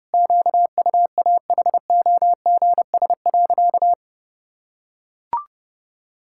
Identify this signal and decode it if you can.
Morse code, heard as a keyed tone, about 32 wpm, 700 Hz. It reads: QUAHOGS.
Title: Pangram Challenge v2 - 30wpm